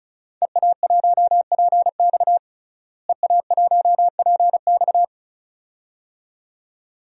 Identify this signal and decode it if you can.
Morse code, heard as a keyed tone, about 36 wpm, 700 Hz. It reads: EA1PX EA1PX
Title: DXCC-100 Call Signs - 5 Characters - Repeated 2x - 35wpm